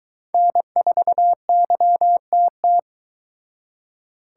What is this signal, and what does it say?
Morse code, heard as a keyed tone, about 21 wpm, 700 Hz. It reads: N4YTT
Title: Speed Racing - USA 1x3 Call Signs 17wpm